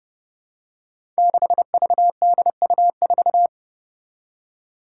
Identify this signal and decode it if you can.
Morse code, heard as a keyed tone, about 30 wpm, 700 Hz. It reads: 6VDU4
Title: Five Challenging Letter-Number Combinations 30wpm